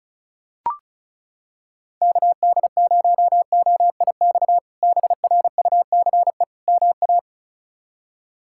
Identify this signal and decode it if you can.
Morse code, heard as a keyed tone, about 34 wpm, 700 Hz. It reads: KD0OIX BRUCE MA
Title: K1USN SST QSO Practice - 35wpm